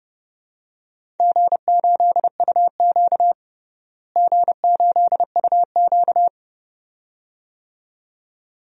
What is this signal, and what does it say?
Morse code, heard as a keyed tone, about 32 wpm, 700 Hz. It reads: G8UQ G8UQ
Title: DXCC-100 Call Signs - 4 Characters - Repeated 2x - 30wpm